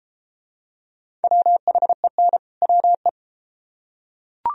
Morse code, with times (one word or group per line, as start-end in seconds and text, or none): 1.2-2.4 WHEN
2.6-3.1 WE